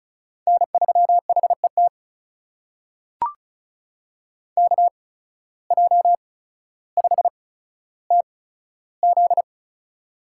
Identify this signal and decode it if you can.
Morse code, heard as a keyed tone, about 36 wpm, spaced as for 15 wpm, 690 Hz. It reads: N3HET KJ5TZ
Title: USA Call Signs - 5 Characters - 8x Character Spacing - 35wpm